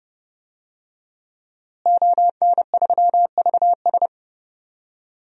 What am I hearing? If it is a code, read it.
Morse code, heard as a keyed tone, about 30 wpm, 700 Hz. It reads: ON3VS
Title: DXCC-100 Call Signs - 5 Characters - Repeated 3x - 30wpm